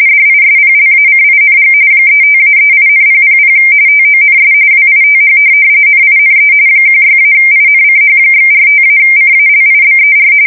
RTTY, about 45.45 baud. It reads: DISPLACING 35-38,000 TONS WITH NINE 16-INCH/45-CALIBER
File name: fsk170 / USS-New-Jersey